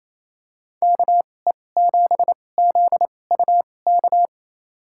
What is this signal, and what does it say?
Morse code, heard as a keyed tone, about 28 wpm, spaced as for 20 wpm, 700 Hz. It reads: KE7ZUK
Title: USA Call Signs - 6 Characters - 2x Character Spacing - 28wpm